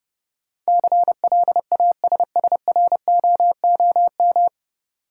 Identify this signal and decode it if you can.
Morse code, heard as a keyed tone, about 32 wpm, 700 Hz. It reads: CLASSROOM
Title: Sentences from Top 1500 Words - New Word Drill - 30wpm